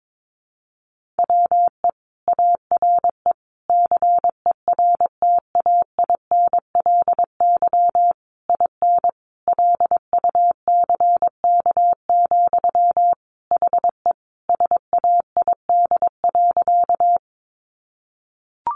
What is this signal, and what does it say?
Morse code, heard as a keyed tone, about 22 wpm, 700 Hz. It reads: WE ARE CERTAINLY IN LUCK, HE SAID.
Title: Sentences from Top 1500 Words 22wpm